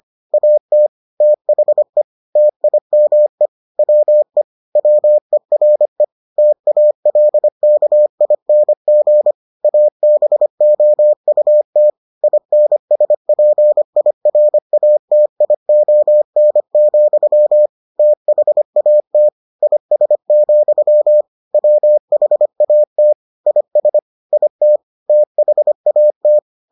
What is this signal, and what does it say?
Morse code, heard as a keyed tone, about 25 wpm, 600 Hz. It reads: AT THE TIME WE WERE TALKING ABOUT INSPIRATION, THAT IS, WHAT IS IT THAT